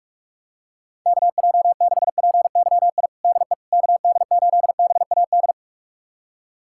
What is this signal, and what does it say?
Morse code, heard as a keyed tone, about 42 wpm, 700 Hz. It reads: KJ6PYI DE KD8BAD
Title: USA Callsign DE Callsign - 6 Characters - 45wpm